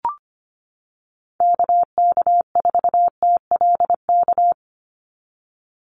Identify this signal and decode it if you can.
Morse code, heard as a keyed tone, about 25 wpm, 700 Hz. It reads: KK4TLK
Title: USA Call Signs 25wpm